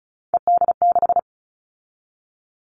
Morse code, heard as a keyed tone, about 34 wpm, 700 Hz. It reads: ED6
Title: Callsign Prefixes for Top 50 DXCC - 35wpm